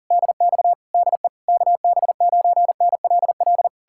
Morse code, heard as a keyed tone, about 53 wpm, spaced as for 47 wpm, 690 Hz. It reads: DX DE KB9NLL
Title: Speed-Racing Common QSO Phrases Encore - 40wpm